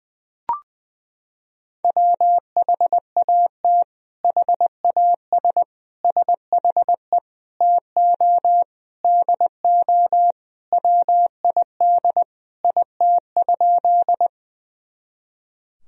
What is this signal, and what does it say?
Morse code, heard as a keyed tone, about 21 wpm, 700 Hz. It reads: WHAT HAS SHE TO DO WID IT?
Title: Sentences from Top 100 Words and Abbreviations 20wpm